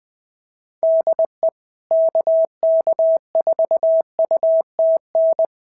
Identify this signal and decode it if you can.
Morse code, heard as a keyed tone, about 20 wpm, 690 Hz. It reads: DE KK4UTN
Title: SOTA QSO Practice 20wpm